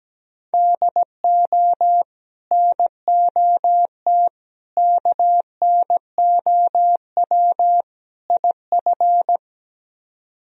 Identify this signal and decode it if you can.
Morse code, heard as a keyed tone, about 17 wpm, 700 Hz. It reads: DO NOT KNOW IF
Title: Speed-Racing - Sets of 4 Words 17wpm